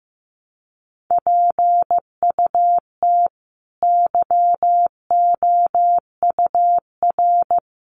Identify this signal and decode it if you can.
Morse code, heard as a keyed tone, about 15 wpm, 700 Hz. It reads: PUT YOUR
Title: Sentences from Top 1100 Words 15wpm